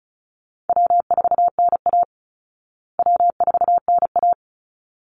Morse code, heard as a keyed tone, about 34 wpm, 700 Hz. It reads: W4NA W4NA
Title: Easy Call Signs Repeated 2x - 35wpm